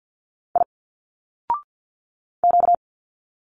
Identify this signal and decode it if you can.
Morse code, heard as a keyed tone, about 50 wpm, 700 Hz. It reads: I Q
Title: Single Letters 50wpm